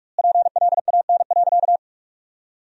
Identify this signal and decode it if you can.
Morse code, heard as a keyed tone, about 44 wpm, 700 Hz. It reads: PLAN.
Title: 3x Word-Spacing - Sentences from Top 500 Words - 45wpm